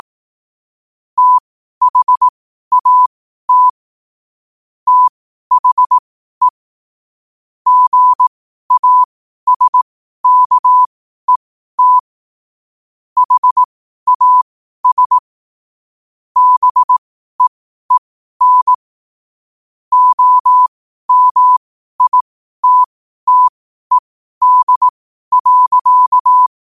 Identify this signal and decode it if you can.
Morse code, heard as a keyed tone, about 18 wpm, spaced as for 12 wpm, 1000 Hz. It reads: THAT THE GASKET HAS BEEN OMITTED.
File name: qotdfiles / 12wpm